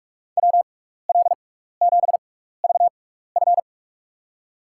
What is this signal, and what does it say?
Morse code, heard as a keyed tone, about 44 wpm, spaced as for 15 wpm, 700 Hz. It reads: WL7VF
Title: USA Call Signs - 5 Characters - 6x Character Spacing - 45wpm